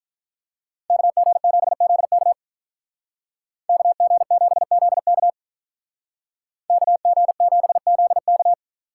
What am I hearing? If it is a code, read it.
Morse code, heard as a keyed tone, about 43 wpm, 700 Hz. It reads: KG7ZK KG7ZK KG7ZK
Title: Speed Racing - USA 2x2 Call Signs 35wpm